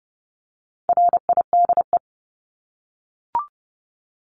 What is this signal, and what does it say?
Morse code, heard as a keyed tone, about 30 wpm, 700 Hz. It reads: RIDE
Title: Top 1001-1250 Words 30wpm